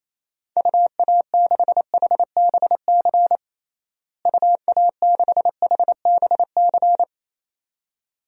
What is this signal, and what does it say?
Morse code, heard as a keyed tone, about 28 wpm, 700 Hz. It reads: UA6HBC UA6HBC
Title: DXCC-100 Call Signs - 6 Characters - Repeated 2x - 28wpm